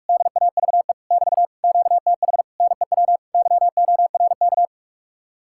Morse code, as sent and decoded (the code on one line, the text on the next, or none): -.. .- ...- . / -...- / --.- - .... / -. .-- / -.-- --- .-. -.-
DAVE <BT> QTH NEW YORK